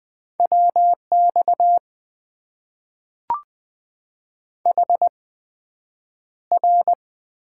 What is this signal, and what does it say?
Morse code, heard as a keyed tone, about 20 wpm, spaced as for 9 wpm, 700 Hz. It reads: WX HR
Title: Top 90 QSO Elements - 8x Character Spacing - 20wpm